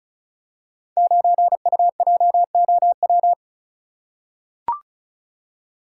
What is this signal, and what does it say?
Morse code, heard as a keyed tone, about 36 wpm, 700 Hz. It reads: 9UJOW
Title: Code Groups - 5 Letter:Numbers - 35wpm